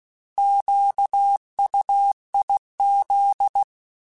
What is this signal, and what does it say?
Morse code, heard as a keyed tone, about 16 wpm, 800 Hz. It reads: QUIZ